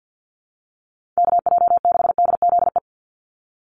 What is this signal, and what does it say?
Morse code, heard as a keyed tone, about 50 wpm, 700 Hz. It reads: KJ6DZE
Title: USA Call Signs 50wpm